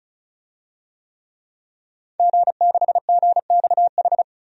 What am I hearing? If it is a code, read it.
Morse code, heard as a keyed tone, about 35 wpm, 700 Hz. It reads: G6GXH
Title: DXCC-100 Call Signs - 5 Characters - 35wpm